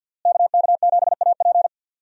Morse code, heard as a keyed tone, about 52 wpm, 700 Hz. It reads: KK7AP
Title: USA Call Signs - 5 Characters - Repeated 3x - 50wpm